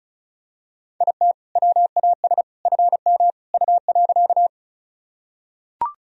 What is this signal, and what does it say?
Morse code, heard as a keyed tone, about 34 wpm, 700 Hz. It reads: IT WAS FM U.
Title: Sentences from Top 100 Words and Abbreviations 35wpm